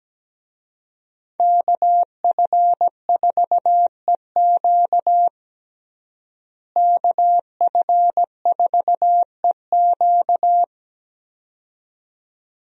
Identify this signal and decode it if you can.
Morse code, heard as a keyed tone, about 17 wpm, 700 Hz. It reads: KF4EQ KF4EQ
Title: USA Call Signs - 5 Characters - Repeated 2x - 17wpm